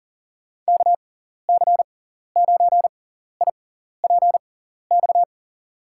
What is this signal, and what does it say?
Morse code, heard as a keyed tone, about 43 wpm, spaced as for 14 wpm, 700 Hz. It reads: KC9IPX
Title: USA Call Signs - 6 Characters - 6x Character Spacing - 40wpm